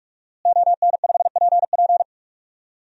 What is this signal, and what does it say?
Morse code, heard as a keyed tone, about 43 wpm, 700 Hz. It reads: ON5PP
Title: DXCC-100 Call Signs - 5 Characters - Repeated 2x - 45wpm